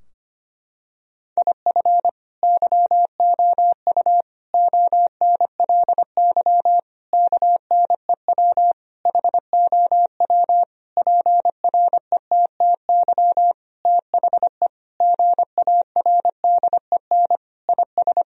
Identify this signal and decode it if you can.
Morse code, heard as a keyed tone, about 25 wpm, 700 Hz. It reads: IF YOU ONLY KNEW HOW PRETTY THE GARDEN IS
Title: Sentences from Top 1000 Words - Review Part 4 - 25wpm